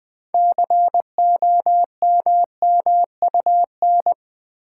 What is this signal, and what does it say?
Morse code, heard as a keyed tone, about 21 wpm, 700 Hz. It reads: COMMUN
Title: Build A Word - 3 Letters At A Time - 9-Letter Words - 20wpm